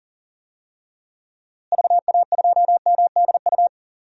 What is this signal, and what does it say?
Morse code, heard as a keyed tone, about 37 wpm, 700 Hz. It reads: VA2MDU
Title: DXCC-100 Call Signs - 6 Characters - Repeated 3x - 40wpm